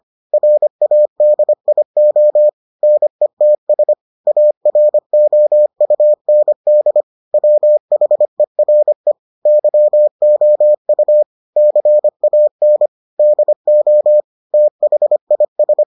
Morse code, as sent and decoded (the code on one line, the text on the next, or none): .-. .- -.. .. --- / -. - ... / .- .-. --- ..- -. -.. / .-- .... .-. / -.-- --- ..- / -.-. .- -. / -.. --- / - .... .. ...
RADIO NETS AROUND WHERE YOU CAN DO THIS